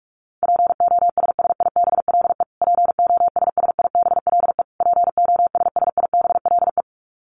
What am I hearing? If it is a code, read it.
Morse code, heard as a keyed tone, about 44 wpm, 700 Hz. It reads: POSSIBLE POSSIBLE POSSIBLE
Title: Top 500 Words Repeated 45wpm